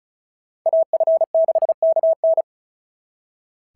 Morse code, heard as a keyed tone, about 36 wpm, 690 Hz. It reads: AF6KN
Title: SOTA QSO Practice 35wpm